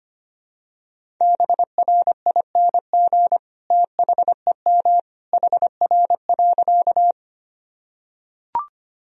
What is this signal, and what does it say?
Morse code, heard as a keyed tone, about 25 wpm, 700 Hz. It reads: BRING THEM HR.
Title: Sentences from Top 500 Words and Abbreviations 25wpm